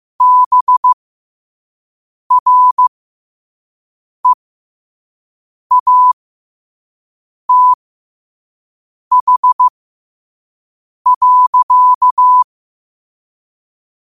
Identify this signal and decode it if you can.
Morse code, heard as a keyed tone, about 15 wpm, spaced as for 5 wpm, 1000 Hz. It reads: BREATH.
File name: qotdfiles / 5wpm